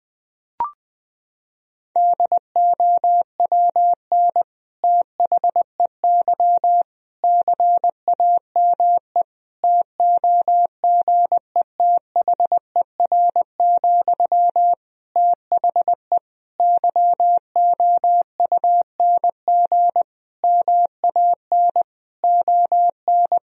Morse code, heard as a keyed tone, about 20 wpm, 700 Hz. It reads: DOWN THEY CAME TOGETHER, THE YOUNG MAN ON
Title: Sentences from Top 1000 Words - Review Part 2 - 20wpm